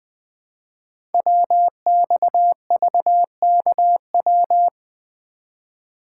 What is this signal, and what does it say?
Morse code, heard as a keyed tone, about 20 wpm, 700 Hz. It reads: WXVKW
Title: Code Groups - 5 Letters - 20wpm